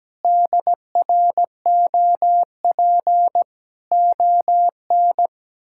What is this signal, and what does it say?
Morse code, heard as a keyed tone, about 17 wpm, 700 Hz. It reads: DROP ON
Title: Pangram Challenge v2 - 17wpm